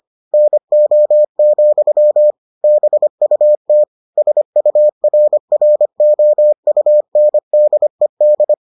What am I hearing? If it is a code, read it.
Morse code, heard as a keyed tone, about 25 wpm, 600 Hz. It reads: NO, BUT SURROUNDED